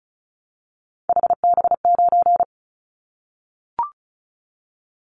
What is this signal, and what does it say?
Morse code, heard as a keyed tone, about 36 wpm, 700 Hz. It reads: HB9
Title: Callsign Prefixes for Top 25 DXCC - 35wpm